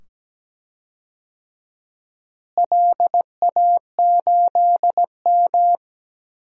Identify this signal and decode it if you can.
Morse code, heard as a keyed tone, about 17 wpm, 700 Hz. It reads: LA8M
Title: DXCC-100 Call Signs - 4 Characters - Repeated 2x - 17wpm